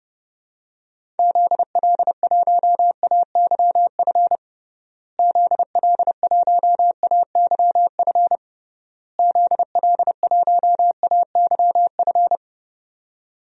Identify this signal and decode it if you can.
Morse code, heard as a keyed tone, about 30 wpm, 700 Hz. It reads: ZL1AYF ZL1AYF ZL1AYF
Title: DXCC-100 Call Signs - 6 Characters - Repeated 3x - 30wpm